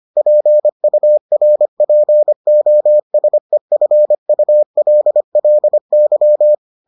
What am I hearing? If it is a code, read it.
Morse code, heard as a keyed tone, about 25 wpm, 600 Hz. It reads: PURPOSEFULLY